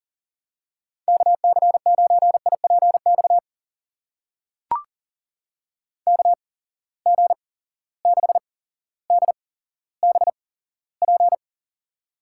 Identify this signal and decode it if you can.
Morse code, heard as a keyed tone, about 37 wpm, spaced as for 17 wpm, 700 Hz. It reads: KC9IPX KG6DBP
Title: USA Call Signs - 6 Characters - 8x Character Spacing - 40wpm